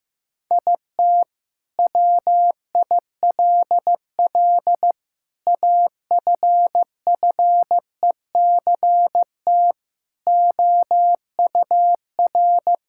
Morse code, heard as a keyed tone, about 16 wpm, 700 Hz. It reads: IT WILL AFFECT OUR
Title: Sentences from Top 1000 Words - Review Part 6 - 15wpm